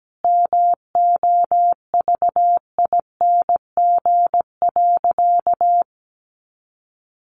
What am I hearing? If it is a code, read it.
Morse code, heard as a keyed tone, about 17 wpm, 700 Hz. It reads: MOVING.
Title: A Sound Of Thunder 17wpm